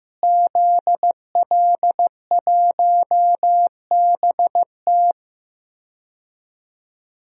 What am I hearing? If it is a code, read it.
Morse code, heard as a keyed tone, about 15 wpm, 700 Hz. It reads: ZL1BT
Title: DXCC-100 Call Signs - 5 Characters - Repeated 2x - 15wpm